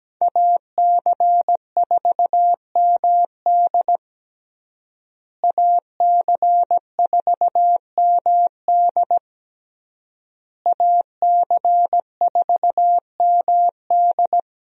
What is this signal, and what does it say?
Morse code, heard as a keyed tone, about 17 wpm, 700 Hz. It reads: AC4MD AC4MD AC4MD
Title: USA Call Signs - 5 Characters - Repeated 3x - 17wpm